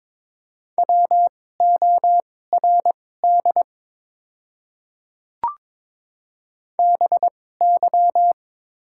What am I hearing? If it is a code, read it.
Morse code, heard as a keyed tone, about 22 wpm, spaced as for 16 wpm, 700 Hz. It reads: WORD BY
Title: 2x Character Spacing - Top 100 Words - 22wpm